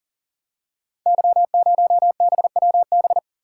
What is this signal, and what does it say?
Morse code, heard as a keyed tone, about 39 wpm, 700 Hz. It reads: Y0BWB
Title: Code Groups - 5 Letter:Numbers - 40wpm